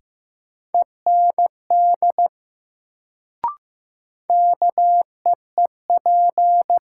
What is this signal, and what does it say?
Morse code, heard as a keyed tone, about 15 wpm, 700 Hz. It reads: END KEEP
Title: Sets of 2 Words - New Word Drill - 15wpm